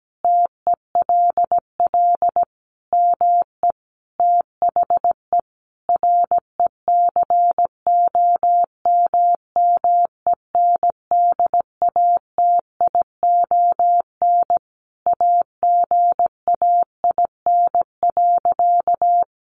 Morse code, heard as a keyed tone, about 17 wpm, 700 Hz. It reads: TELL ME THE RECOMMENDATION AGAIN.
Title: Sentences from Top 1400 Words 17wpm